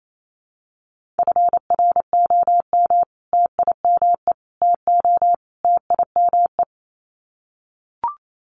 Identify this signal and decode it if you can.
Morse code, heard as a keyed tone, about 28 wpm, 700 Hz. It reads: FROM TIME TO TIME
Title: Sets of 4 Words 28wpm